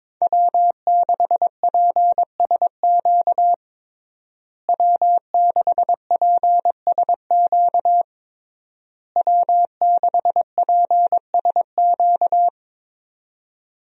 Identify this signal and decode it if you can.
Morse code, heard as a keyed tone, about 22 wpm, 700 Hz. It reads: W6PSQ W6PSQ W6PSQ
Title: USA Call Signs - 5 Characters - Repeated 3x - 22wpm